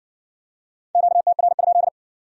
Morse code, heard as a keyed tone, about 57 wpm, 700 Hz. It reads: QTR?